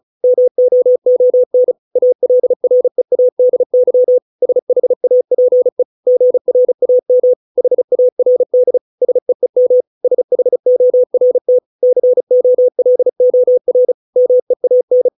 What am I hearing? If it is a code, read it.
Morse code, heard as a keyed tone, about 34 wpm, 500 Hz. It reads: MOON ALREADY SHAPE GRAM HARD SEEM SHORT COLOR MEAN
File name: 632 words / 632 Words 35WPM